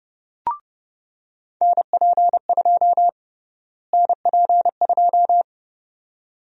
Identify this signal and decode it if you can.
Morse code, heard as a keyed tone, about 30 wpm, 700 Hz. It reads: NP2 NP2
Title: USA Call Sign Prefixes - Repeated 2x - 30wpm